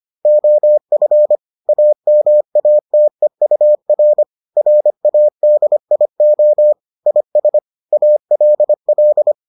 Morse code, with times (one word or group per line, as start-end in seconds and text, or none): 0.3-1.4 OF
1.7-4.2 AMATEUR
4.6-6.7 RADIO
7.1-7.6 IS
7.9-9.3 ALL